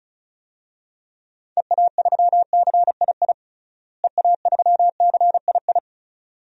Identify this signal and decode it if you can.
Morse code, heard as a keyed tone, about 36 wpm, 700 Hz. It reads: EA3CII EA3CII
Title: DXCC-100 Call Signs - 6 Characters - Repeated 2x - 35wpm